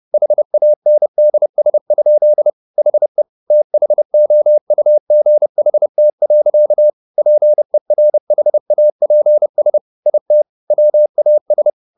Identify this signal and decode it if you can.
Morse code, heard as a keyed tone, about 30 wpm, 600 Hz. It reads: HANDS? HE THOUGHT. PERHAPS IT WAS